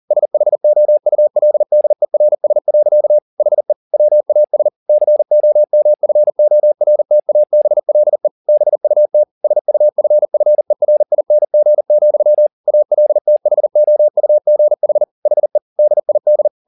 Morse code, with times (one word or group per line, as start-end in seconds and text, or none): 0.1-3.2 SHOULDERS.
3.4-3.7 HE
3.9-4.7 WAS
4.9-8.3 COMFORTABLE
8.5-9.2 BUT
9.4-12.5 SUFFERING,
12.7-15.0 ALTHOUGH
15.3-15.6 HE
15.8-16.5 DID